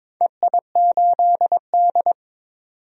Morse code, heard as a keyed tone, about 22 wpm, 700 Hz. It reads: EI8D
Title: DXCC-100 Call Signs - 4 Characters - 22wpm